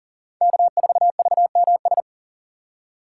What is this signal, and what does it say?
Morse code, heard as a keyed tone, about 37 wpm, 700 Hz. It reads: K4VMS
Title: USA Call Signs - 5 Characters - 4x Character Spacing - 40wpm